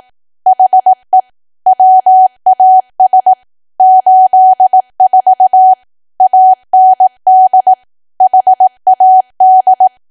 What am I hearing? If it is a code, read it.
Morse code, heard as a keyed tone, about 18 wpm, 750 Hz. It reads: HE WAS 84 AND HAD
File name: Archive / 18 WPM / 18WPM